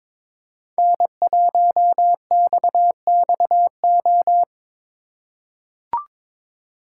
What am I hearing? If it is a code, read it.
Morse code, heard as a keyed tone, about 23 wpm, 700 Hz. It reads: N1XXO
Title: USA Call Signs - 5 Characters - 2x Character Spacing - 22wpm